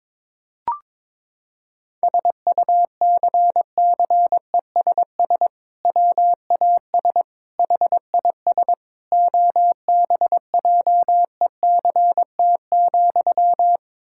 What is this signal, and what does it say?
Morse code, heard as a keyed tone, about 22 wpm, 700 Hz. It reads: SUCCESS WAS HIS OBJECT,
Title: Sentences from Top 1000 Words - Review Part 6 - 22wpm